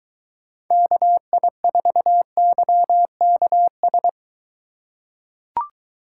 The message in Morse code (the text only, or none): KI4YKS